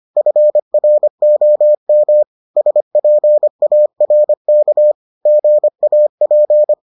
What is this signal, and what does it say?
Morse code, heard as a keyed tone, about 25 wpm, 600 Hz. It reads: FROM SPARK GAP